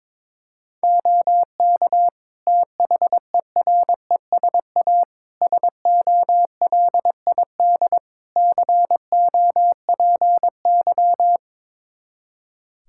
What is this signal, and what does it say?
Morse code, heard as a keyed tone, about 22 wpm, 700 Hz. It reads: OK THERESA SOLID COPY
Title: Practice QSO - Exchange 3 - More Info - 22wpm